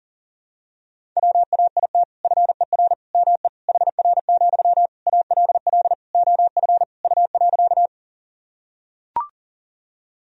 Morse code, heard as a keyed tone, about 37 wpm, 700 Hz. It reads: WAIT FER ME HR, ALL OF U.
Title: Sentences from Top 500 Words and Abbreviations 40wpm